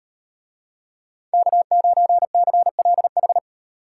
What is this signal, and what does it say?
Morse code, heard as a keyed tone, about 38 wpm, 700 Hz. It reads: K9CLH